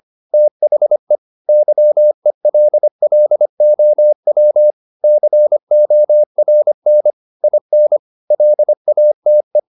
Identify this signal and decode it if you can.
Morse code, heard as a keyed tone, about 25 wpm, 600 Hz. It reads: THE YELLOW CORN IN LATE
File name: SS111 25